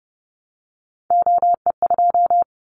Morse code, heard as a keyed tone, about 30 wpm, 700 Hz. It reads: OE2
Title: Callsign Prefixes for Top 100 DXCC - 30wpm